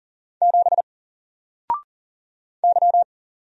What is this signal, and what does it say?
Morse code, heard as a keyed tone, about 40 wpm, 700 Hz. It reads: 7 Y
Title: Single Letter-Number 40wpm